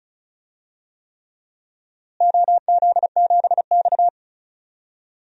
Morse code, heard as a keyed tone, about 36 wpm, 700 Hz. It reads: OZ7X
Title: DXCC-100 Call Signs - 4 Characters - Repeated 3x - 35wpm